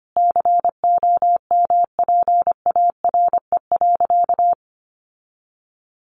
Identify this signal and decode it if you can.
Morse code, heard as a keyed tone, about 25 wpm, 700 Hz. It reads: COMPARE.
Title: Sentences from Top 1300 Words 25wpm